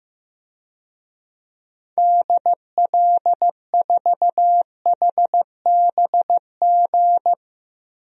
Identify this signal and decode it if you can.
Morse code, heard as a keyed tone, about 15 wpm, 700 Hz. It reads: DL4HBG
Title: DXCC-100 Call Signs - 6 Characters - 15wpm